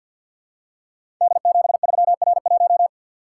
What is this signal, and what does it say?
Morse code, heard as a keyed tone, about 46 wpm, 700 Hz. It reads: D73R1
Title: Code Groups - 5 Letter:Numbers - 50wpm